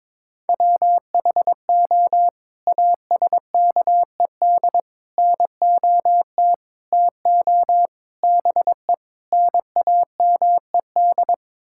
Morse code, heard as a keyed tone, about 22 wpm, 700 Hz. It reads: WHO ASKED NOT TO BE NAMED
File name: practice / 22wpm / Sets of 7 Words - 22wpm